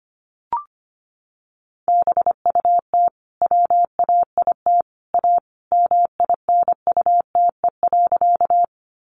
Morse code, heard as a keyed tone, about 25 wpm, 700 Hz. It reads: BUT WAIT A MINUTE.
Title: Sentences from Top 500 Words 25wpm